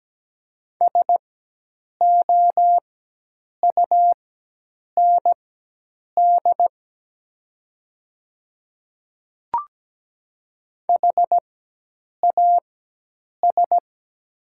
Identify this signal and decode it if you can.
Morse code, heard as a keyed tone, about 17 wpm, spaced as for 8 wpm, 700 Hz. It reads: SOUND HAS
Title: 4x Character Spacing - Top 100 Words - 17wpm